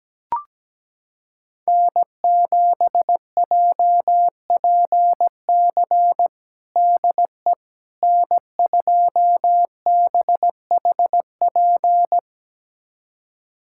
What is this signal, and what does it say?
Morse code, heard as a keyed tone, about 17 wpm, 700 Hz. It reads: N7JPC DE N2BHP
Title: USA Callsign DE Callsign - 5 Characters - 17wpm